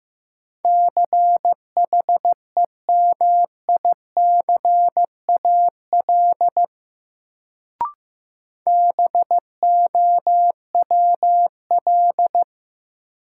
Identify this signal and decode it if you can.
Morse code, heard as a keyed tone, about 15 wpm, 700 Hz. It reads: CHEMICAL BOWL
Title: Sentences from Top 1200 Words - New Word Drill - 15wpm